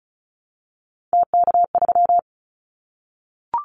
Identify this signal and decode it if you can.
Morse code, heard as a keyed tone, about 36 wpm, 700 Hz. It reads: TK3
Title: Callsign Prefixes for Top 100 DXCC - 35wpm